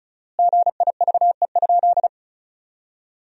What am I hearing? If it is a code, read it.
Morse code, heard as a keyed tone, about 36 wpm, 700 Hz. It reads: GIVE?
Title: 4x Word-Spacing - Sentences from Top 200 Words - 35wpm